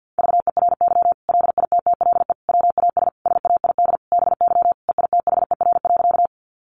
Morse code, heard as a keyed tone, about 50 wpm, 700 Hz. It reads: VERY LITTLE WAS SAID BY EITHER.
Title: Sentences from Top 600 Words 50wpm